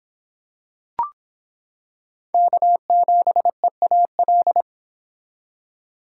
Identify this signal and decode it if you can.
Morse code, heard as a keyed tone, about 25 wpm, 700 Hz. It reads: K7EAL